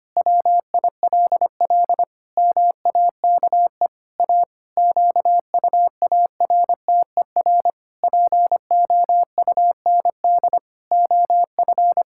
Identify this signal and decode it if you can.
Morse code, heard as a keyed tone, about 25 wpm, 700 Hz. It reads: WILL MAKE A QUARTER POUND OF
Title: Pangram Challenge v2 - 25wpm